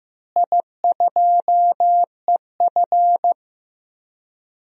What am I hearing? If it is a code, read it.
Morse code, heard as a keyed tone, about 15 wpm, 700 Hz. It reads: I2EF
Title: DXCC-100 Call Signs - 4 Characters - Repeated 3x - 15wpm